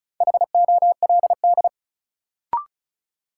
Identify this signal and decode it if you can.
Morse code, heard as a keyed tone, about 36 wpm, 700 Hz. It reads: HOLD
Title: Sentences from Top 400 Words - New Word Drill - 35wpm